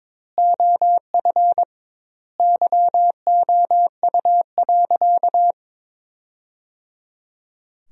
Morse code, heard as a keyed tone, about 21 wpm, 700 Hz. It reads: OF YOU.
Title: 2x Word-Spacing - Sentences from Top 400 Words - 22wpm